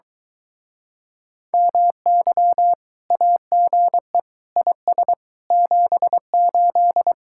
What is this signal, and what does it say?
Morse code, heard as a keyed tone, about 23 wpm, 700 Hz. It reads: MY AGE IS 78